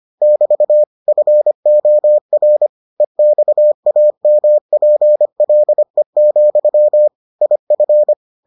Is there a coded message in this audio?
Yes: Morse code from 0.2 to 8.1 s, heard as a keyed tone, about 25 wpm, 600 Hz. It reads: <BT> FOR EXAMPLE, IF